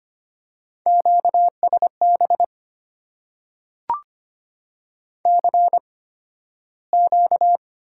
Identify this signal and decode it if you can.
Morse code, heard as a keyed tone, about 25 wpm, spaced as for 13 wpm, 690 Hz. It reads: QSB CQ
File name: Top 90 QSO Elements - 8x Character Spacing - 25wpm